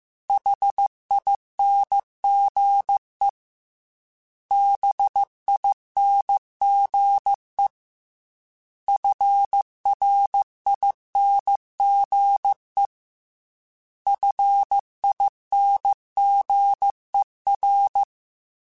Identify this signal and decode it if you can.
Morse code, heard as a keyed tone, about 15 wpm, 800 Hz. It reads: HINGE BINGE FRINGE FINGER